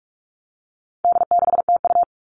Morse code, heard as a keyed tone, about 44 wpm, 700 Hz. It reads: D6TU
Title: All Cut Numbers - 4-Digit 45wpm